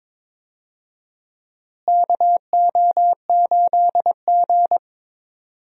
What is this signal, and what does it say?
Morse code, heard as a keyed tone, about 22 wpm, 700 Hz. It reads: KO8G
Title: USA Call Signs - 4 Characters - Repeated 2x - 22wpm